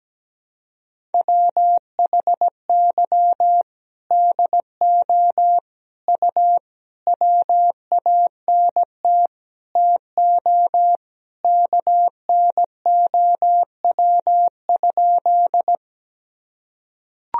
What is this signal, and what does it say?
Morse code, heard as a keyed tone, about 17 wpm, 700 Hz. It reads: WHY DO U WANT TO KNOW?
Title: Sentences from Top 200 Words and Abbreviations 17wpm